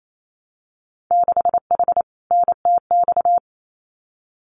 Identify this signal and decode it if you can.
Morse code, heard as a keyed tone, about 28 wpm, 700 Hz. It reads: 6H NTX
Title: Winter Field Day Exchanges Encore 28wpm